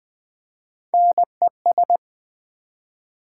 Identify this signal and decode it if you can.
Morse code, heard as a keyed tone, about 20 wpm, 700 Hz. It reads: NES
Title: Build A Word - 1 Letter At A Time - 4-Letter Words - 20wpm